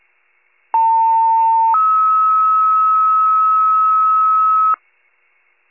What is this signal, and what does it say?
Two-tone page, heard as a one-second tone, then a three-second tone, about 900 then 1290 Hz, with noise. It is heard from 0.7 to 4.7 s.